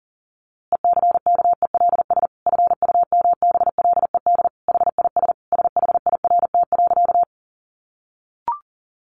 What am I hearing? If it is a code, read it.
Morse code, heard as a keyed tone, about 40 wpm, 700 Hz. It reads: ECKELS FUMBLED HIS SHIRT.
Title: A Sound Of Thunder 40wpm